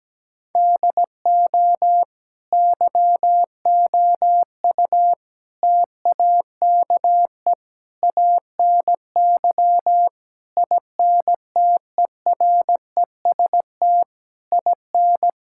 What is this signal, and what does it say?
Morse code, heard as a keyed tone, about 17 wpm, 700 Hz. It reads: DO YOU TAKE ANY INTEREST IN
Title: Sentences from Top 1000 Words - Review Part 6 - 17wpm